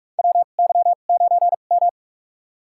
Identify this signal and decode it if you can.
Morse code, heard as a keyed tone, about 46 wpm, spaced as for 33 wpm, 700 Hz. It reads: WY9M
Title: USA Call Signs - 2x Character Spacing - 45wpm